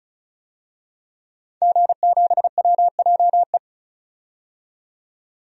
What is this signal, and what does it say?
Morse code, heard as a keyed tone, about 34 wpm, 700 Hz. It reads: G7WJE
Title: DXCC-100 Call Signs - 5 Characters - 35wpm